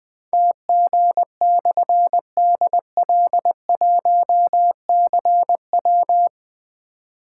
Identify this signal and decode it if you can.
Morse code, heard as a keyed tone, about 20 wpm, 700 Hz. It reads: TG/DL1CW
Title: Difficult Call Signs 20wpm